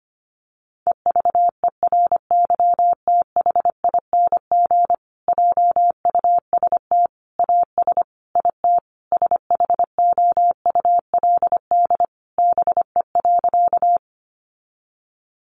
Morse code, heard as a keyed tone, about 25 wpm, 700 Hz. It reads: EVERYTHING JUST AS IT SHOULD BE.
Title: Sentences from Top 700 Words 25wpm